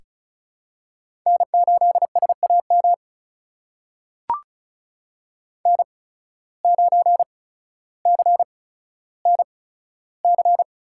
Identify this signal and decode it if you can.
Morse code, heard as a keyed tone, about 36 wpm, spaced as for 15 wpm, 700 Hz. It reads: N8SAM N9CNC
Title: USA Call Signs - 5 Characters - 8x Character Spacing - 35wpm